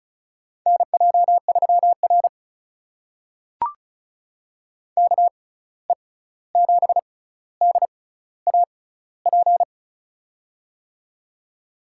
Morse code, heard as a keyed tone, about 34 wpm, spaced as for 16 wpm, 700 Hz. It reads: NJ3R KE7DAP
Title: USA Call Signs - 6x Character Spacing - 35wpm